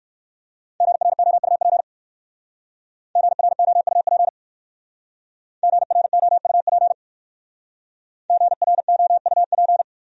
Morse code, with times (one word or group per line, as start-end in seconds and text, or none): 0.8-1.8 GROUP
3.2-4.3 GROUP
5.6-6.9 GROUP
8.3-9.8 GROUP